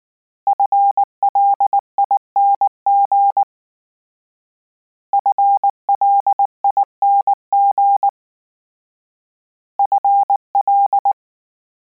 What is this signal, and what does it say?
Morse code, heard as a keyed tone, about 19 wpm, 800 Hz. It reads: FLING FLING FL